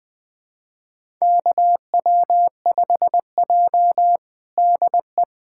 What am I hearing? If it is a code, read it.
Morse code, heard as a keyed tone, about 20 wpm, 700 Hz. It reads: KW5J DE
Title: USA Callsign DE Callsign - 4 Characters - 20wpm